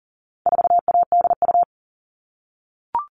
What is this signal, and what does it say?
Morse code, heard as a keyed tone, about 40 wpm, 700 Hz. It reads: 4ADU
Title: All Cut Numbers - 4-Digit 40wpm